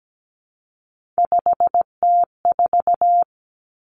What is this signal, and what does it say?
Morse code, heard as a keyed tone, about 17 wpm, 700 Hz. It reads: 5T4
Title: Three Challenging Letter-Number Combinations 17wpm